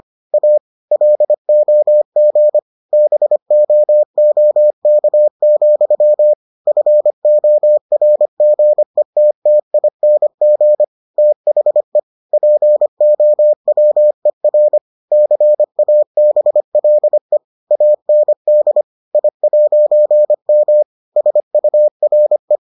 Morse code, heard as a keyed tone, about 25 wpm, 600 Hz. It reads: A LOG BOOK, FORGETTING THE POWER CABLE AND I'M SURE